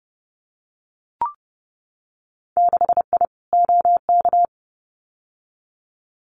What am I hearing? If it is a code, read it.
Morse code, heard as a keyed tone, about 30 wpm, 700 Hz. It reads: 6I OK
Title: Winter Field Day Exchanges Encore 30wpm